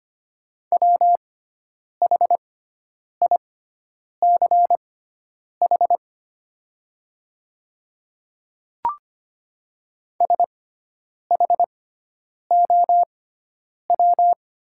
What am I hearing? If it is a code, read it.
Morse code, heard as a keyed tone, about 25 wpm, spaced as for 9 wpm, 690 Hz. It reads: WHICH SHOW
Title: 6x Character Spacing - Top 200 Words - 25wpm